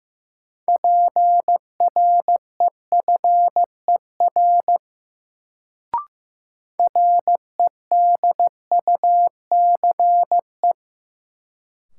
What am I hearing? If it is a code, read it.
Morse code, heard as a keyed tone, about 16 wpm, 700 Hz. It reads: PREFER REDUCE
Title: Sentences from Top 1100 Words - New Word Drill - 15wpm